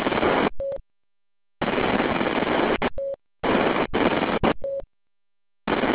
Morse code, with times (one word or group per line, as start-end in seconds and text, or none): none